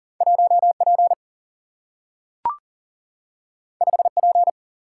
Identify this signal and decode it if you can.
Morse code, heard as a keyed tone, about 38 wpm, 700 Hz. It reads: 1 P 5 P
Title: Single-Letter Single-Number 40wpm